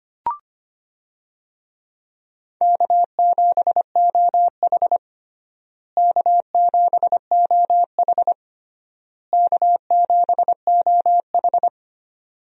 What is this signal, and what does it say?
Morse code, heard as a keyed tone, about 25 wpm, 700 Hz. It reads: K7OH K7OH K7OH
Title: USA Call Signs - 4 Characters - Repeated 3x - 25wpm